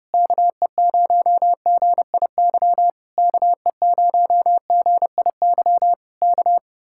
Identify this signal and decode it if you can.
Morse code, heard as a keyed tone, about 28 wpm, 700 Hz. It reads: KE0GIY KE0GIY K
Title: Speed-Racing Common QSO Phrases Encore - 30wpm